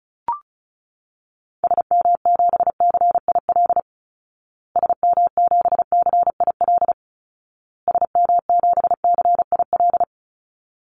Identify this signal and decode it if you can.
Morse code, heard as a keyed tone, about 34 wpm, 700 Hz. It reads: SM7CIL SM7CIL SM7CIL
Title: Call Signs Repeated 2x - 35wpm